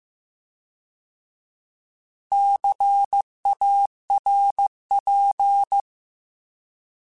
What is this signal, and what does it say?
Morse code, heard as a keyed tone, about 15 wpm, 800 Hz. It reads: CARP